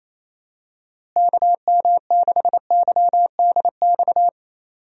Morse code, heard as a keyed tone, about 28 wpm, 700 Hz. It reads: KM6YDX
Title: USA Call Signs - 6 Characters - Repeated 3x - 28wpm